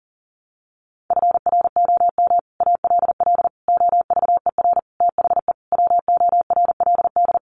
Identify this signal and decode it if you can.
Morse code, heard as a keyed tone, about 40 wpm, 700 Hz. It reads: FROM ALL OVER THE WORLD
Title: Sets of 5 Words 40wpm